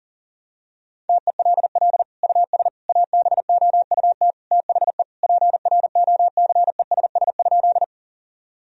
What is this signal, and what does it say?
Morse code, heard as a keyed tone, about 40 wpm, 700 Hz. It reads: TELL US ABOUT THE PROCESS?
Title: Sentences from Top 1000 Words - Review Part 3 - 40wpm